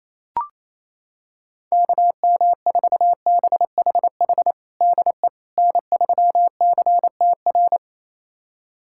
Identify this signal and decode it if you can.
Morse code, heard as a keyed tone, about 28 wpm, 700 Hz. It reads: KM4BHH DE N3CTR
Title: USA Callsign DE Callsign 28wpm